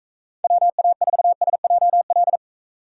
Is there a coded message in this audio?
Yes: Morse code from 0.4 to 2.4 s, heard as a keyed tone, about 43 wpm, 700 Hz. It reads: WA4SJL